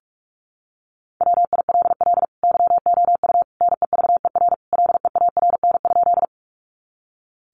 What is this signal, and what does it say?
Morse code, heard as a keyed tone, about 44 wpm, 700 Hz. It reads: WILL YOU NEVER LEARN?
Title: Sentences from Top 300 Words 45wpm